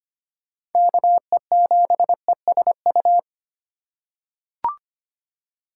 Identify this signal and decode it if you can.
Morse code, heard as a keyed tone, about 25 wpm, 700 Hz. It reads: KE7ESU
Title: USA Call Signs - 6 Characters - 3x Character Spacing - 25wpm